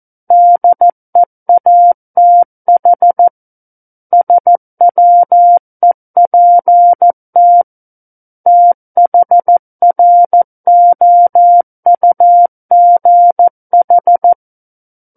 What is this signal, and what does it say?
Morse code, heard as a keyed tone, about 14 wpm, 700 Hz. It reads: DEATH SWEPT THROUGH
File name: Book One / 05-Chapter 5 - 14 WPM